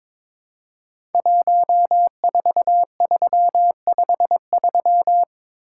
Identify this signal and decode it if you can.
Morse code, heard as a keyed tone, about 22 wpm, 700 Hz. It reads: 14353
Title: Code Groups - 5 Letter:Numbers - 22wpm